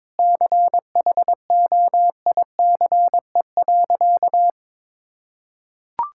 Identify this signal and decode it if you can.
Morse code, heard as a keyed tone, about 22 wpm, 700 Hz. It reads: CHOICE.
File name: practice / 22wpm / Sentences with 4-Word Phrases - 22wpm